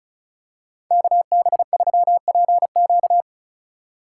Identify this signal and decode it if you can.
Morse code, heard as a keyed tone, about 34 wpm, 700 Hz. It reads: KB3PQ
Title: USA Call Signs - 5 Characters - 3x Character Spacing - 35wpm